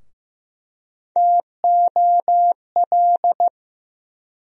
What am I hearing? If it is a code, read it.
Morse code, heard as a keyed tone, about 15 wpm, 700 Hz. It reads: TOL
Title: USA Call Sign Postfixes - 3 Letters 15wpm